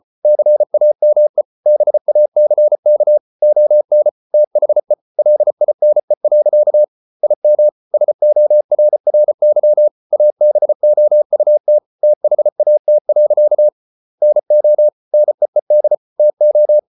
Morse code, heard as a keyed tone, about 34 wpm, 600 Hz. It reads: CAME BACK ON THE LINE. IM SORRY ABOUT THAT. NO NEED TO